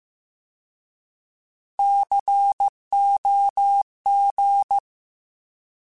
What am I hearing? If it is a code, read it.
Morse code, heard as a keyed tone, about 15 wpm, 800 Hz. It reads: COG